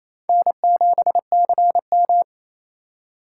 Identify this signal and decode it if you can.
Morse code, heard as a keyed tone, about 28 wpm, 700 Hz. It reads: N7CM
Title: USA Call Signs - 4 Characters - 3x Character Spacing - 28wpm